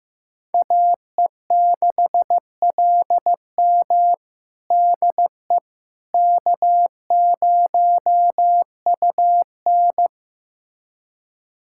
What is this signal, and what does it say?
Morse code, heard as a keyed tone, about 15 wpm, 700 Hz. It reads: AE6LM DE K0UN
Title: USA Callsign DE Callsign 15wpm